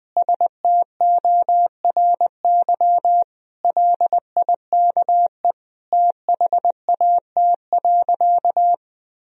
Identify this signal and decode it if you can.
Morse code, heard as a keyed tone, about 20 wpm, 700 Hz. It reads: STORY LIKE THAT.
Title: Sentences from Top 1000 Words - Review Part 3 - 20wpm